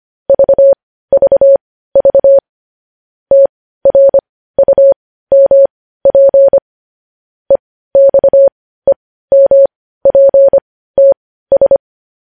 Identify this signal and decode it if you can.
Morse code, heard as a keyed tone, about 25 wpm, spaced as for 15 wpm, 550 Hz. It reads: VVV TRUMP EXEMPTS
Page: News Headlines in Morse Code at 15 WPM